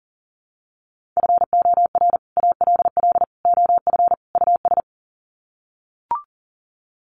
Morse code, heard as a keyed tone, about 40 wpm, 700 Hz. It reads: FOR ALL OF US
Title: Sets of 4 Words 40wpm